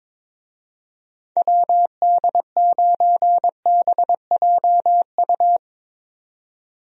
Morse code, heard as a keyed tone, about 22 wpm, 700 Hz. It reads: WD9BJU
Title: USA Call Signs - 6 Characters - 22wpm